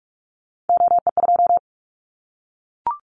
Morse code, heard as a keyed tone, about 44 wpm, 700 Hz. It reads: OE2
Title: Callsign Prefixes for Top 25 DXCC - 45wpm